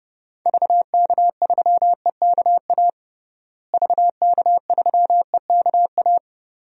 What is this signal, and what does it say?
Morse code, heard as a keyed tone, about 29 wpm, 700 Hz. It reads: VK3EKA VK3EKA
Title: DXCC-100 Call Signs - 6 Characters - Repeated 2x - 30wpm